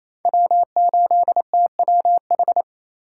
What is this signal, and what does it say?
Morse code, heard as a keyed tone, about 28 wpm, 700 Hz. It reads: W8TWH
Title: USA Call Signs - 2x Character Spacing - 28wpm